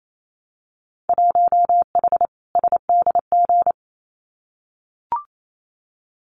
Morse code, heard as a keyed tone, about 28 wpm, 700 Hz. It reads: 1H SDG
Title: Winter Field Day Exchanges 28wpm